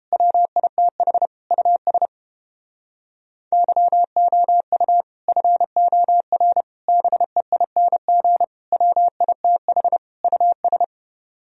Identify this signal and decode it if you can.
Morse code, heard as a keyed tone, about 29 wpm, 700 Hz. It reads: WITH US YOU FOR BEING WITH US